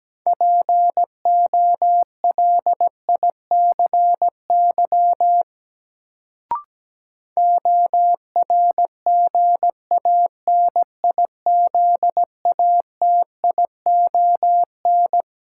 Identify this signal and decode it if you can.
Morse code, heard as a keyed tone, about 17 wpm, 700 Hz. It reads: POLICY ORGANIZATION
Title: Sentences from Top 1000 Words - Review Part 5 - New Word Drill - 17wpm